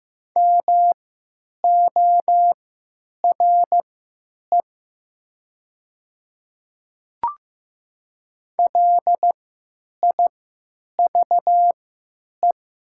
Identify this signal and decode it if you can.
Morse code, heard as a keyed tone, about 15 wpm, spaced as for 9 wpm, 700 Hz. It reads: MORE LIVE
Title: 3x Character Spacing - Top 200 Words - 15wpm